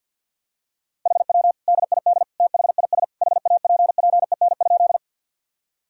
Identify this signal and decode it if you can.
Morse code, heard as a keyed tone, about 50 wpm, 700 Hz. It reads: HW DID THIS HAPPEN?
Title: Sentences from Top 400 Words and Abbreviations 50wpm